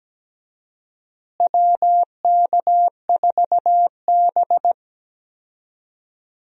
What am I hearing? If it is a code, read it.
Morse code, heard as a keyed tone, about 17 wpm, 700 Hz. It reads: WK4B
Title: USA Call Signs - 4 Characters - Repeated 3x - 17wpm